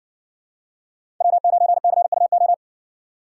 Morse code, heard as a keyed tone, about 60 wpm, 700 Hz. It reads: W9CUO